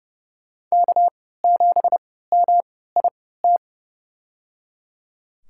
Morse code, heard as a keyed tone, about 30 wpm, spaced as for 17 wpm, 700 Hz. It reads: K7MIT
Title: USA Call Signs - 5 Characters - 3x Character Spacing - 30wpm